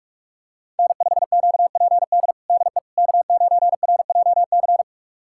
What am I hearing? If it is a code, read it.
Morse code, heard as a keyed tone, about 41 wpm, 700 Hz. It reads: N5QPD DE K9RJC
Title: USA Callsign DE Callsign - 5 Characters - 45wpm